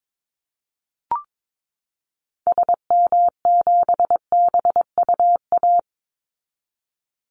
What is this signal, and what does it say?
Morse code, heard as a keyed tone, about 22 wpm, 700 Hz. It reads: SM7BUA
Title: Call Signs Repeated 2x - 22wpm